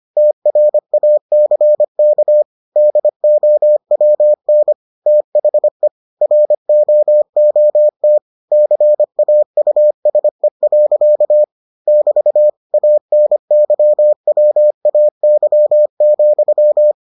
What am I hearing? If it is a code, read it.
Morse code, heard as a keyed tone, about 25 wpm, 600 Hz. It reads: TRACK DOWN THE ROOT CAUSE. <BT> ANYWAY,